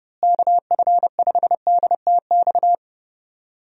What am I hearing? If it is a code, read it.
Morse code, heard as a keyed tone, about 31 wpm, 700 Hz. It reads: KF5DTX
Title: USA Call Signs - 6 Characters - Repeated 3x - 30wpm